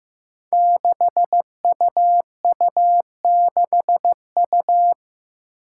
Morse code, heard as a keyed tone, about 15 wpm, 700 Hz. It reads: 6UU6U
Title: Five Challenging Letter-Number Combinations 15wpm